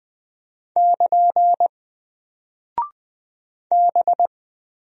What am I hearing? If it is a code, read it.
Morse code, heard as a keyed tone, about 20 wpm, 700 Hz. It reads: <KN> B
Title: Single Character (Letter:Number:Punctuation:Prosign) Alt 20wpm